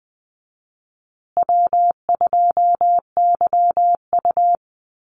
Morse code, heard as a keyed tone, about 20 wpm, 700 Hz. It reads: W2YU
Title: USA Call Signs Repeated 3x 20wpm